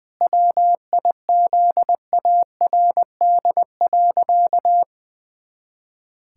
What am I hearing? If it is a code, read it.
Morse code, heard as a keyed tone, about 20 wpm, 700 Hz. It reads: WIZARD.
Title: Pangram Challenge v2 - 20wpm